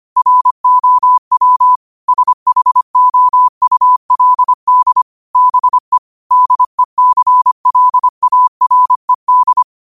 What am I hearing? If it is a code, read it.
Morse code, heard as a keyed tone, about 25 wpm, 1000 Hz. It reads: ROW SHOULD BE DECLARED
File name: qotdfiles / 25wpm